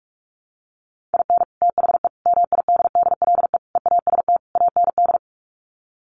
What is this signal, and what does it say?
Morse code, heard as a keyed tone, about 44 wpm, 700 Hz. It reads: IN THE MIDDLE EAST AND
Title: Sets of 5 Words Encore 45wpm